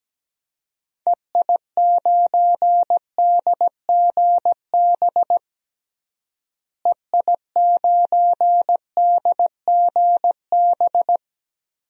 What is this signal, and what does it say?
Morse code, heard as a keyed tone, about 17 wpm, 700 Hz. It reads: EI9DGB EI9DGB
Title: DXCC-100 Call Signs - 6 Characters - Repeated 2x - 17wpm